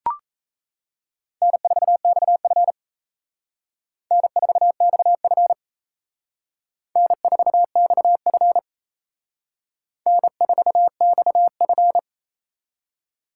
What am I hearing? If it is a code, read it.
Morse code, heard as a keyed tone, about 35 wpm, 700 Hz. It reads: N4XF N4XF N4XF N4XF
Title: Speed Racing - USA 1x2 Call Signs 28wpm